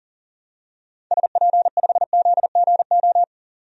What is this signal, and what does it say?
Morse code, heard as a keyed tone, about 38 wpm, 700 Hz. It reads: SP5ZGO
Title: DXCC-100 Call Signs - 6 Characters - 40wpm